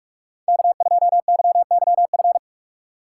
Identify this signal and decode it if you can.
Morse code, heard as a keyed tone, about 46 wpm, 700 Hz. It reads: K2YYF
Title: USA Call Signs - 5 Characters - 8x Character Spacing - 45wpm